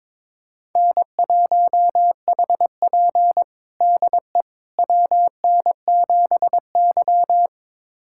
Morse code, heard as a keyed tone, about 22 wpm, 700 Hz. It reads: N1HP DE WN7Y
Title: USA Callsign DE Callsign - 4 Characters - 22wpm